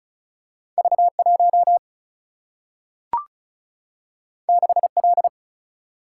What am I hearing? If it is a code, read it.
Morse code, heard as a keyed tone, about 36 wpm, 700 Hz. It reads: V1 6L
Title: Single-Letter Single-Number 35wpm